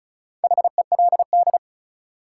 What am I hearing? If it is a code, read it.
Morse code, heard as a keyed tone, about 36 wpm, 700 Hz. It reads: HELD
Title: Sentences from Top 1600 Words - New Word Drill - 35wpm